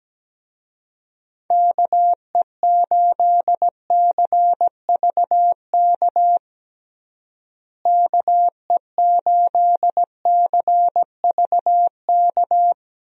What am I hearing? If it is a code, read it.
Morse code, heard as a keyed tone, about 17 wpm, 700 Hz. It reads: KE8CVK KE8CVK
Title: USA Call Signs - 6 Characters - Repeated 2x - 17wpm